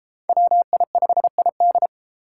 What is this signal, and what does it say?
Morse code, heard as a keyed tone, about 32 wpm, 700 Hz. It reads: WI5ID